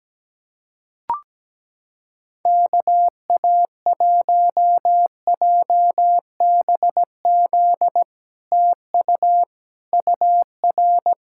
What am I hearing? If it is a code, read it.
Morse code, heard as a keyed tone, about 17 wpm, 700 Hz. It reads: KA1JBZ TU UR
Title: SOTA QSO Practice 17wpm